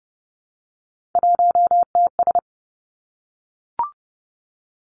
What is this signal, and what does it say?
Morse code, heard as a keyed tone, about 30 wpm, 700 Hz. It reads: 1TS
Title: Three Challenging Letter-Number Combinations 30wpm